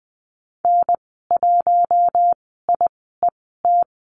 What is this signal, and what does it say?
Morse code, heard as a keyed tone, about 20 wpm, spaced as for 14 wpm, 700 Hz. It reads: N1IET
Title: 076 Lesson 13g - Letter L - Call Signs 20wpm